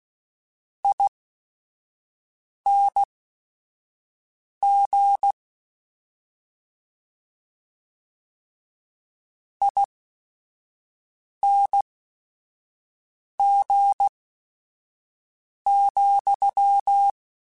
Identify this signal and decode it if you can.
Morse code, heard as a keyed tone, about 16 wpm, spaced as for 5 wpm, 800 Hz. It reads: ING ING,